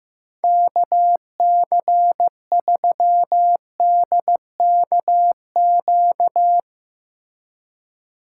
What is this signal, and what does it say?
Morse code, heard as a keyed tone, about 16 wpm, 700 Hz. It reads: KC3DKQ
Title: Speed Racing - USA 2x3 Call Signs 15wpm